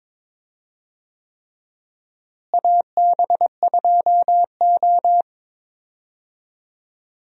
Morse code, heard as a keyed tone, about 22 wpm, 700 Hz. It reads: AB2O
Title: USA Call Signs - 4 Characters - Repeated 3x - 22wpm